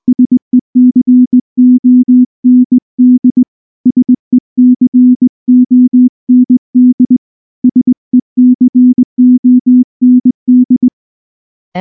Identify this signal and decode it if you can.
Morse code, heard as a keyed tone, about 19 wpm, 250 Hz. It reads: SECOND SECOND SECOND
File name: morse / morse-training-20WPM-2